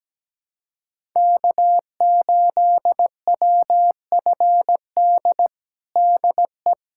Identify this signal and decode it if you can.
Morse code, heard as a keyed tone, about 17 wpm, 700 Hz. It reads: K8WFD DE
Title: USA Callsign DE Callsign - 5 Characters - 17wpm